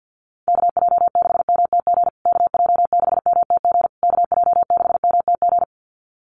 Morse code, heard as a keyed tone, about 50 wpm, 700 Hz. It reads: KJ6MTG KJ6MTG KJ6MTG
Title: USA Call Signs Repeated 3x 50wpm